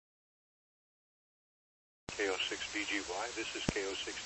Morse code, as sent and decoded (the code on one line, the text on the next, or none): none